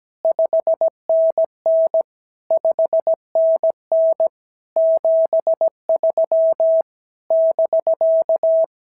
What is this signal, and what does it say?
Morse code, heard as a keyed tone, about 17 wpm, 650 Hz. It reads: 5NN 5NN 73 <BK>
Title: SOTA QSO Practice 17wpm